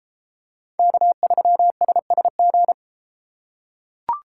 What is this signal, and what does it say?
Morse code, heard as a keyed tone, about 33 wpm, 700 Hz. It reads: K3SSG